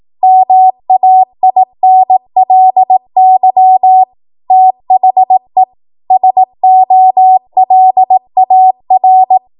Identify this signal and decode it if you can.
Morse code, heard as a keyed tone, about 18 wpm, 750 Hz. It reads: MAINLY THE SOLAR
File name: Morse / 18WPM